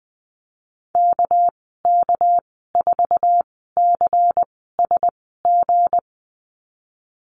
Morse code, heard as a keyed tone, about 20 wpm, spaced as for 15 wpm, 700 Hz. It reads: KK4CSG
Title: 200 Lesson 32g - Letter k - Call Signs 20wpm